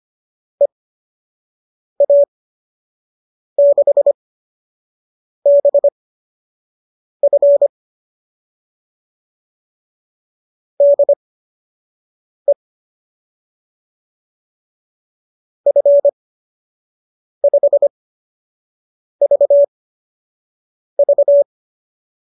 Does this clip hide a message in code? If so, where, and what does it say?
Yes: Morse code from 0.6 to 21.4 s, heard as a keyed tone, about 25 wpm, spaced as for 6 wpm, 590 Hz. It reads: EA6BF DE F5VV